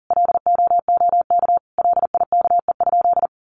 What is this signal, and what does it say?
Morse code, heard as a keyed tone, about 40 wpm, 700 Hz. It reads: LOOK LIKE?
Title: Sentences from Top 100 Words 40wpm